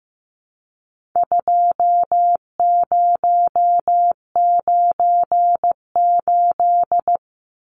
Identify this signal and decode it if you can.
Morse code, heard as a keyed tone, about 15 wpm, 700 Hz. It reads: 2098
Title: Numbers - 4-Digit 15wpm